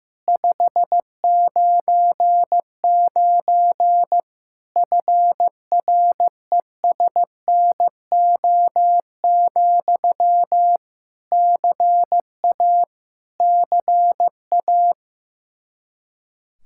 Morse code, heard as a keyed tone, about 15 wpm, 700 Hz. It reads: 599 FRESNO, CA CA
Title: Practice QSO - Exchange 1 - Beginning - 15wpm